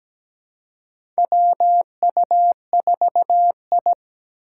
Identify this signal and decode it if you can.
Morse code, heard as a keyed tone, about 17 wpm, 700 Hz. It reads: WU4I
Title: USA Call Signs - 3x Character Spacing - 17wpm